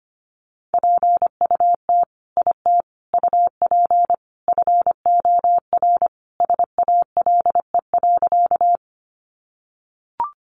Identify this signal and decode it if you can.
Morse code, heard as a keyed tone, about 25 wpm, 700 Hz. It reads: PUT IT UP FOR SALE.
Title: Sentences from Top 1400 Words 25wpm